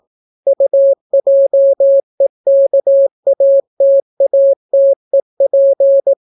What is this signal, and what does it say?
Morse code, heard as a keyed tone, about 18 wpm, 550 Hz. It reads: UJEKATATEP